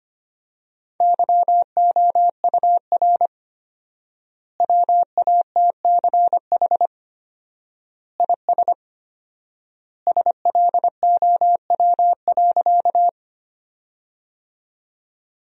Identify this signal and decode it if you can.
Morse code, heard as a keyed tone, about 25 wpm, 700 Hz. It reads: YOUR WATCH IS SLOW.
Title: 4x Word-Spacing - Sentences from Top 400 Words - 25wpm